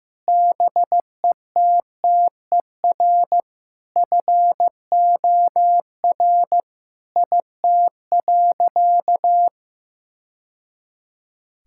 Morse code, heard as a keyed tone, about 15 wpm, 700 Hz. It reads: BETTER FOR IT.
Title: Sentences from Top 1000 Words - Review Part 5 - 15wpm